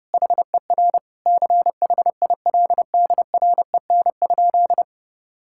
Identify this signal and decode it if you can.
Morse code, heard as a keyed tone, about 30 wpm, 700 Hz. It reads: HER CHILDREN?
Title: Sentences from Top 1000 Words - Review Part 5 - 30wpm